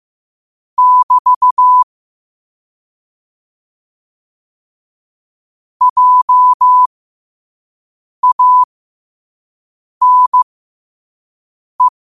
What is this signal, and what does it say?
Morse code, heard as a keyed tone, about 15 wpm, spaced as for 5 wpm, 1000 Hz. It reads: <BT> JANE
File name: qotdfiles / 5wpm